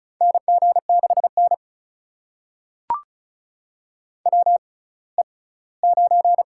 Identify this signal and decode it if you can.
Morse code, heard as a keyed tone, about 36 wpm, spaced as for 20 wpm, 700 Hz. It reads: NG6N WE9
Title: USA Call Signs - 4 Characters - 6x Character Spacing - 35wpm